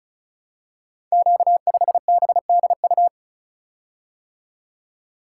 Morse code, heard as a keyed tone, about 34 wpm, 700 Hz. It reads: Q5BDU
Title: Code Groups - 5 Letter:Numbers - 35wpm